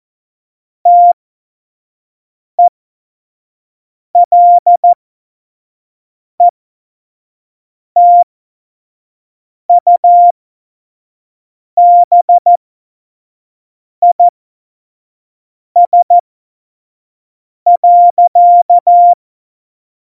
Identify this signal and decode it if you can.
Morse code, heard as a keyed tone, about 14 wpm, spaced as for 5 wpm, 700 Hz. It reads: TELETUBIS.
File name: Morsekurs / Witze / Witz 16. Januar 2018, 25 F70